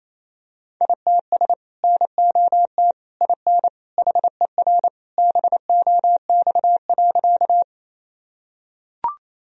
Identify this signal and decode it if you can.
Morse code, heard as a keyed tone, about 28 wpm, 700 Hz. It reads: ITS NOT IN HER BOX.
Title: Sentences from Top 1000 Words - Review Part 2 - 28wpm